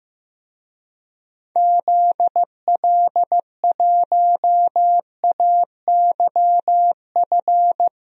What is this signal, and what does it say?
Morse code, heard as a keyed tone, about 15 wpm, 700 Hz. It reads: ZL1AYF
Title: DXCC-100 Call Signs - 6 Characters - Repeated 3x - 15wpm